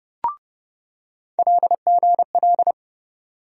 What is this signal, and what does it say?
Morse code, heard as a keyed tone, about 30 wpm, 700 Hz. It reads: LGL
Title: USA Call Sign Postfixes - 3 Letters 30wpm